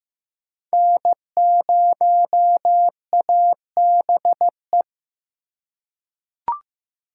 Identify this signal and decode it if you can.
Morse code, heard as a keyed tone, about 15 wpm, 700 Hz. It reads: N0ABE
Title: USA Call Signs - 5 Characters - 4x Character Spacing - 15wpm